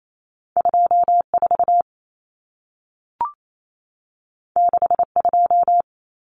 Morse code, heard as a keyed tone, about 28 wpm, 700 Hz. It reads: 24 62
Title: Common Cut Numbers - 2-Digit 28wpm